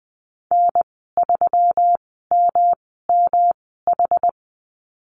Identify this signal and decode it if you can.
Morse code, heard as a keyed tone, about 20 wpm, spaced as for 15 wpm, 700 Hz. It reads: N3MMH
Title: 140 Lesson 23g - Letter f - Call Signs 20wpm